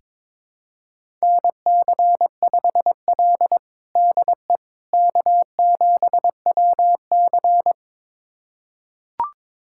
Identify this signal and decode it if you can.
Morse code, heard as a keyed tone, about 22 wpm, 700 Hz. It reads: NC5L DE K7WC
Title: USA Callsign DE Callsign - 4 Characters - 22wpm